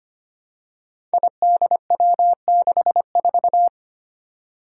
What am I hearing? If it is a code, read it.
Morse code, heard as a keyed tone, about 25 wpm, 700 Hz. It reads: IDW64
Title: Five Challenging Letter-Number Combinations 25wpm